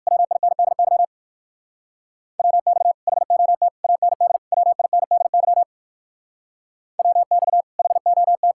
Morse code, heard as a keyed tone, about 54 wpm, 700 Hz. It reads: WINDY WX HOT AND WINDY WX HOT
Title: Speed-Racing Common QSO Phrases Encore - 45wpm